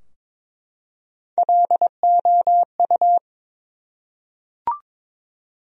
Morse code, heard as a keyed tone, about 22 wpm, 700 Hz. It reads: LOU
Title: USA Common Names - Max 4 Letters - 22wpm